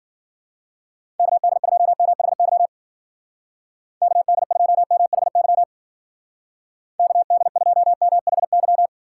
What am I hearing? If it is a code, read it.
Morse code, heard as a keyed tone, about 49 wpm, 700 Hz. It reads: KD2MHY KD2MHY KD2MHY
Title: Speed Racing - USA 2x3 Call Signs 40wpm